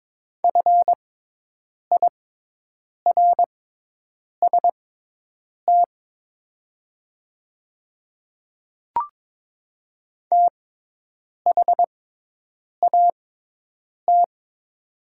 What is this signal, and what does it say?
Morse code, heard as a keyed tone, about 22 wpm, spaced as for 8 wpm, 700 Hz. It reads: FIRST THAT
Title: 6x Character Spacing - Top 100 Words - 22wpm